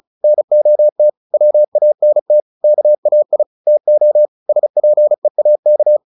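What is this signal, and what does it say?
Morse code, heard as a keyed tone, about 34 wpm, 600 Hz. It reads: NOT WANT KAI TO SPEAK
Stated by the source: Morse practice at 35 wpm from CW med Gnister, sending Ken Follett's novel Never in English.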